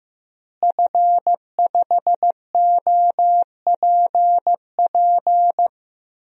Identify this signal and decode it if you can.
Morse code, heard as a keyed tone, about 16 wpm, 700 Hz. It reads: F5OPP
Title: DXCC-100 Call Signs - 5 Characters - Repeated 2x - 15wpm